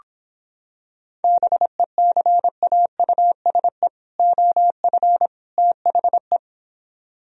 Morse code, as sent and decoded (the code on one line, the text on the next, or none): -... -.-. .- ..- ... / --- ..-. / - .... .
BECAUSE OF THE